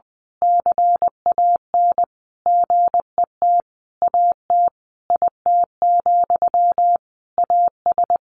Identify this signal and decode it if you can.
Morse code, heard as a keyed tone, about 20 wpm, 700 Hz. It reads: CAN GET AT IT, AS
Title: A Sound Of Thunder 20wpm